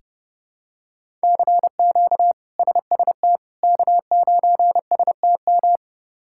Morse code, heard as a keyed tone, about 32 wpm, 700 Hz. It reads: CQ SST K9STM
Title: K1USN SST QSO Practice - 30wpm